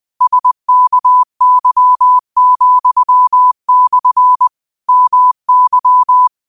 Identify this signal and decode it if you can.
Morse code, heard as a keyed tone, about 20 wpm, 1000 Hz. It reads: SKY,/ MY
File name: qotdfiles / 20wpm